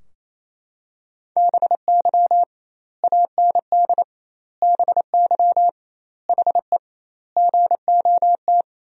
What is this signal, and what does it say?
Morse code, heard as a keyed tone, about 28 wpm, 700 Hz. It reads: BY AND BY HE GOT
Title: 2x Word-Spacing - Sentences from Top 300 Words - 28wpm